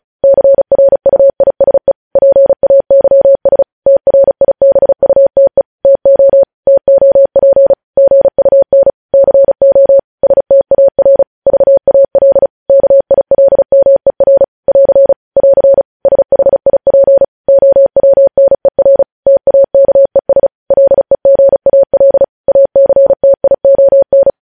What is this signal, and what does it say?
Morse code, heard as a keyed tone, about 36 wpm, 550 Hz. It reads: CRUISE PAYS TRIBUTE TO TOP GUN CO STAR VAL KILMER <AR> <AR> SHIP OWNER TAKES LEGAL ACTION
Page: news headlines in morse code at 35 wpm